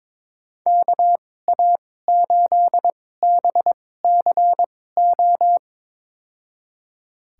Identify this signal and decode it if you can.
Morse code, heard as a keyed tone, about 21 wpm, spaced as for 16 wpm, 700 Hz. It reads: KA8BCO
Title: USA Call Signs - 2x Character Spacing - 22wpm